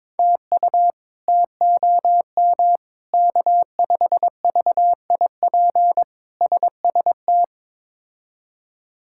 Morse code, heard as a keyed tone, about 22 wpm, 700 Hz. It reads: TU TOM K5VIP SST
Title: K1USN SST QSO Practice - 22wpm